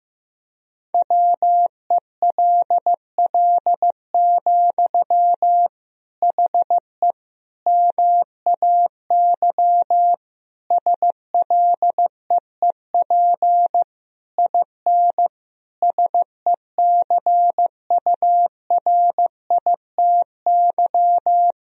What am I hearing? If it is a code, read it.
Morse code, heard as a keyed tone, about 15 wpm, 700 Hz. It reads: WELL, HE MAY SLEEP IN SECURITY
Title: Sentences from Top 1000 Words - Review Part 4 - 15wpm